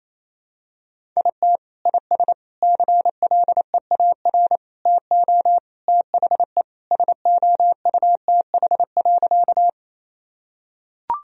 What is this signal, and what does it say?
Morse code, heard as a keyed tone, about 27 wpm, 700 Hz. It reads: IT IS CLEAR TO THE SOUTH.
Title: Sentences from Top 1000 Words - Review Part 2 - 28wpm